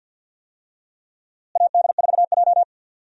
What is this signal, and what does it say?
Morse code, heard as a keyed tone, about 53 wpm, 700 Hz. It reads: AD4J
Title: USA Call Signs - 4 Characters - Repeated 2x - 50wpm